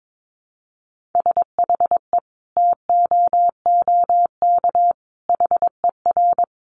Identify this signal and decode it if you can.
Morse code, heard as a keyed tone, about 22 wpm, 700 Hz. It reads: SHE TOOK HER
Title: Sentences from Top 900 Words 22wpm